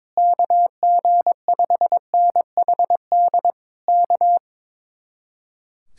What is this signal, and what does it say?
Morse code, heard as a keyed tone, about 22 wpm, 700 Hz. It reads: KG5NHD K
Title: Speed-Racing Common QSO Phrases - 22wpm